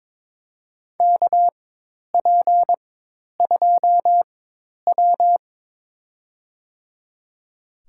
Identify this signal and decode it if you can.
Morse code, heard as a keyed tone, about 22 wpm, spaced as for 10 wpm, 700 Hz. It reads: KP2W
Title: USA Call Signs - 4x Character Spacing - 22wpm